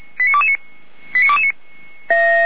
Five-tone selective call (ZVEI), heard as quick runs of five tones, tones of about 70 ms. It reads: 89229